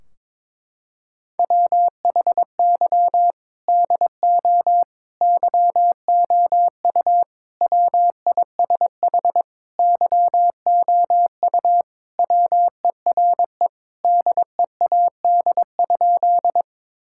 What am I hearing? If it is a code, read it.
Morse code, heard as a keyed tone, about 22 wpm, 700 Hz. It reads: WHY DO YOU WISH YOU WERE DEAD?
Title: Sentences from Top 1600 Words 22wpm